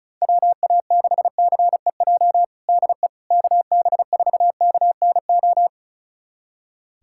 Morse code, heard as a keyed tone, about 34 wpm, 700 Hz. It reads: WA6CEJ DE KB4KNO
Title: USA Callsign DE Callsign - 6 Characters - 35wpm